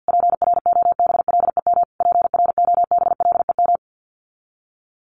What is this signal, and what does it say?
Morse code, heard as a keyed tone, about 50 wpm, 700 Hz. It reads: PROBLEM PROBLEM
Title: Top 500 Words Repeated 50wpm